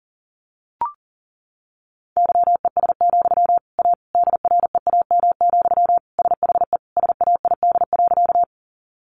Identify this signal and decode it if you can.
Morse code, heard as a keyed tone, about 40 wpm, 700 Hz. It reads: YES, A DREAM, SHE SAID.
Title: Sentences from Top 1500 Words 40wpm